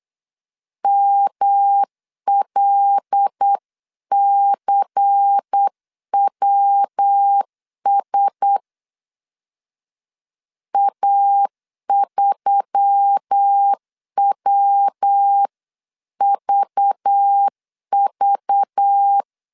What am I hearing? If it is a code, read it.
Morse code, heard as a keyed tone, about 8 wpm, 800 Hz. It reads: MLCWS A3WVV
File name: 8wpm